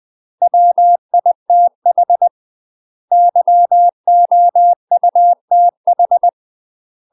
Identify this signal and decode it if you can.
Morse code, heard as a keyed tone, about 20 wpm, 700 Hz. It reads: WITH YOUTH
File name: UK MASTER QSOs TRAINING/20WPM